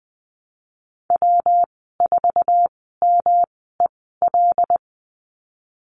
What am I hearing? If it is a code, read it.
Morse code, heard as a keyed tone, about 20 wpm, spaced as for 15 wpm, 700 Hz. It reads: W4MEL
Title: 140 Lesson 23g - Letter f - Call Signs 20wpm